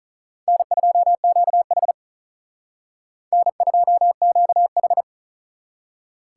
Morse code, heard as a keyed tone, about 43 wpm, 700 Hz. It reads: N2QH N2QH
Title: Speed Racing - USA Call Signs 35wpm